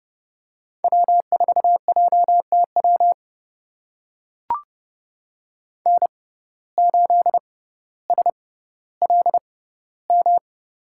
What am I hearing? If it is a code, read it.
Morse code, heard as a keyed tone, about 30 wpm, spaced as for 15 wpm, 690 Hz. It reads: W4JTW N8SLM
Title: USA Call Signs - 5 Characters - 6x Character Spacing - 30wpm